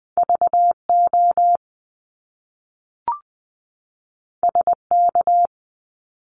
Morse code, heard as a keyed tone, about 20 wpm, 700 Hz. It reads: VO SK
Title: Top Two Letter Combinations in Words 20wpm